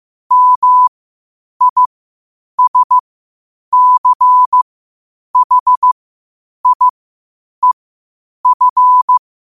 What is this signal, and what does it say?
Morse code, heard as a keyed tone, about 15 wpm, spaced as for 8 wpm, 1000 Hz. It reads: MISCHIEF
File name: qotdfiles / 8wpm